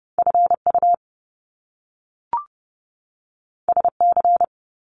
Morse code, heard as a keyed tone, about 30 wpm, 700 Hz. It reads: FU SC
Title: Top Two Letter Combinations in Words 30wpm